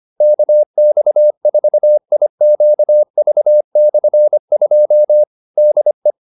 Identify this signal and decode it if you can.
Morse code, heard as a keyed tone, about 25 wpm, 600 Hz. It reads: KX4IQV/2 DE